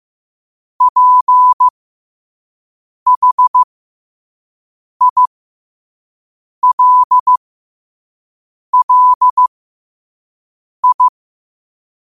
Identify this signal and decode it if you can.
Morse code, heard as a keyed tone, about 15 wpm, spaced as for 5 wpm, 1000 Hz. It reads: PHILLI
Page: Quotes for Sat, 16 Aug 2025 in Morse Code at 5 words per minute.